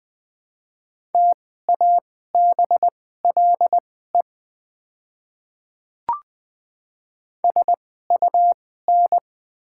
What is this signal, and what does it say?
Morse code, heard as a keyed tone, about 20 wpm, spaced as for 15 wpm, 690 Hz. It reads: TABLE SUN
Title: 2x Character Spacing - Top 500 Words - 20wpm